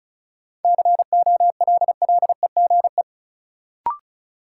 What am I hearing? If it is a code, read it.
Morse code, heard as a keyed tone, about 34 wpm, 700 Hz. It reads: COLLEGE
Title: Sentences from Top 1300 Words - New Word Drill Alt - 35wpm